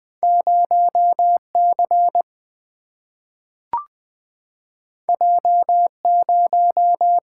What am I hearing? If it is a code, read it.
Morse code, heard as a keyed tone, about 20 wpm, 700 Hz. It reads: 0 C J 0
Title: Single-Letter Single-Number 20wpm